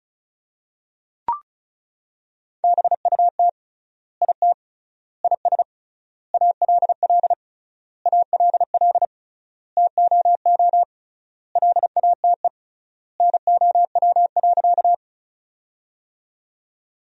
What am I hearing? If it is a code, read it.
Morse code, heard as a keyed tone, about 34 wpm, 700 Hz. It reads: BUT IT IS ALL ALL TOO LATE NOW.
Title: 3x Word-Spacing - Sentences from Top 300 Words - 35wpm